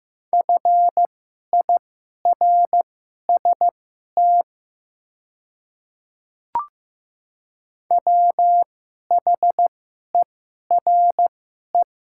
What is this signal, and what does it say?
Morse code, heard as a keyed tone, about 15 wpm, spaced as for 11 wpm, 700 Hz. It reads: FIRST WHERE
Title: 2x Character Spacing - Top 200 Words - 15wpm